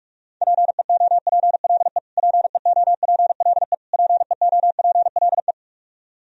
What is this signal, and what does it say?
Morse code, heard as a keyed tone, about 46 wpm, 700 Hz. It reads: PEOPLE PEOPLE PEOPLE
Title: Top 100 Words Repeated 45wpm